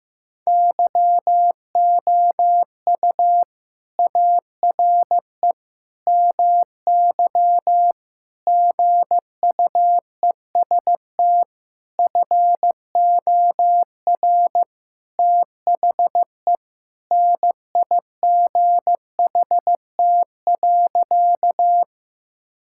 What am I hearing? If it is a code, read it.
Morse code, heard as a keyed tone, about 15 wpm, 700 Hz. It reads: YOU ARE MY GUEST FOR THE NIGHT.
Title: Sentences from Top 1600 Words 15wpm